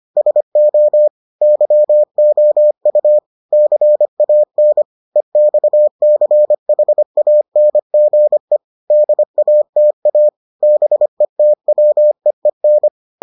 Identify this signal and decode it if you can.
Morse code, heard as a keyed tone, about 25 wpm, 600 Hz. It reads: SO YOU CAN EXCHANGE DATA BETWEEN